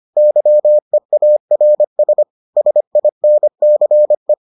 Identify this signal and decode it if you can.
Morse code, heard as a keyed tone, about 25 wpm, 600 Hz. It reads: YEARS SINCE